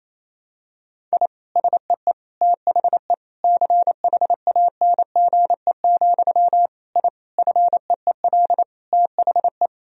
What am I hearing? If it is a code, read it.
Morse code, heard as a keyed tone, about 27 wpm, 700 Hz. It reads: I SEE THE CHANGE, I FEEL THE
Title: Sentences from Top 1000 Words - Review Part 1 - 28wpm